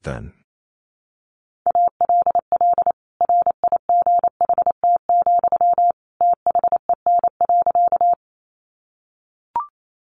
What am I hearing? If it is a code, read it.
Morse code, heard as a keyed tone, about 28 wpm, 700 Hz. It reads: ALL RIGHT, THEN.
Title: Sentences from Top 200 Words 28wpm